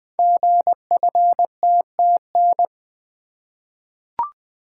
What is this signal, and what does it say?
Morse code, heard as a keyed tone, about 20 wpm, 700 Hz. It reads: GFTTN
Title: Code Groups - 5 Letters - 20wpm